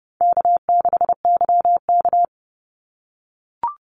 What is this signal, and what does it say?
Morse code, heard as a keyed tone, about 30 wpm, 700 Hz. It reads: K6YK
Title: NA SOTA Chaser Callsigns 30wpm